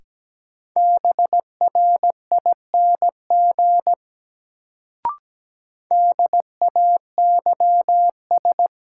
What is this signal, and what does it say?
Morse code, heard as a keyed tone, about 17 wpm, 700 Hz. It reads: BRING DAYS
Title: Sets of 5 Words Encore - New Word Drill - 17wpm